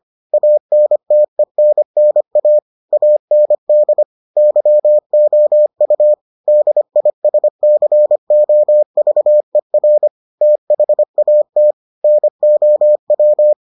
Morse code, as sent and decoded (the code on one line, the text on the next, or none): .- -. - . -. -. .- / .- -. -.. / -.-- --- ..- / -.. .. ... -.-. --- ...- . .-. / - .... .- - / -. --- .--
ANTENNA AND YOU DISCOVER THAT NOW